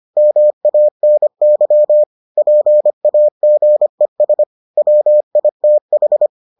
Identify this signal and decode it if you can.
Morse code, heard as a keyed tone, about 25 wpm, 600 Hz. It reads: MANY PAGES WITH